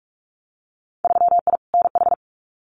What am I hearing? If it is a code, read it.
Morse code, heard as a keyed tone, about 44 wpm, 700 Hz. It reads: 3I NH
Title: Winter Field Day Exchanges Encore 45wpm